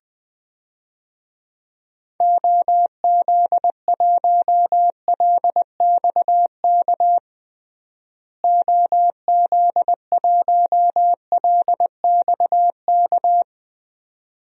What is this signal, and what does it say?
Morse code, heard as a keyed tone, about 20 wpm, 700 Hz. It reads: OZ1LXK OZ1LXK
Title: DXCC-100 Call Signs - 6 Characters - Repeated 2x - 20wpm